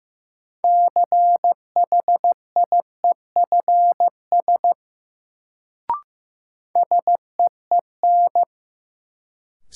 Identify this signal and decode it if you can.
Morse code, heard as a keyed tone, about 15 wpm, 700 Hz. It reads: CHIEFS SEEN
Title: Sets of 5 Words Encore - New Word Drill - 15wpm